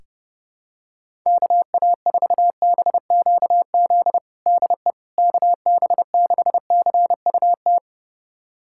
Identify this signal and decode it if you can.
Morse code, heard as a keyed tone, about 30 wpm, 700 Hz. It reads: KA4BQZ DE KB6CUT
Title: USA Callsign DE Callsign - 6 Characters - 30wpm